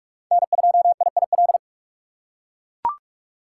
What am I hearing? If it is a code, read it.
Morse code, heard as a keyed tone, about 41 wpm, 700 Hz. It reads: N2IIL
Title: USA Call Signs - 5 Characters - 4x Character Spacing - 45wpm